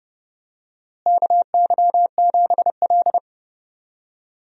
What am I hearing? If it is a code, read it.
Morse code, heard as a keyed tone, about 30 wpm, 700 Hz. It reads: KY7L
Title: USA Call Signs - 4 Characters - Repeated 2x - 30wpm